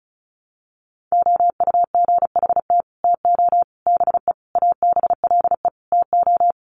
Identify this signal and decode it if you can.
Morse code, heard as a keyed tone, about 34 wpm, 700 Hz. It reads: OUGHT TO BE ABLE TO
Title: Sets of 5 Words Encore2 35wpm